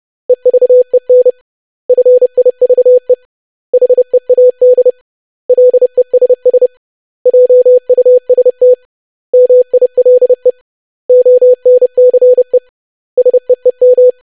Morse code, heard as a keyed tone, about 30 wpm, 500 Hz. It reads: EVEN FIVE HEAD LESS JUST MILE ONCE SEEM
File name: lesson / web / basic500-25wpm